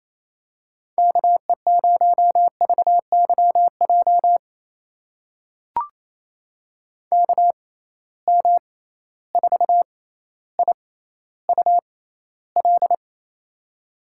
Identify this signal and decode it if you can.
Morse code, heard as a keyed tone, about 28 wpm, spaced as for 14 wpm, 700 Hz. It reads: KE0VYJ KM4IUL
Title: USA Call Signs - 6 Characters - 6x Character Spacing - 28wpm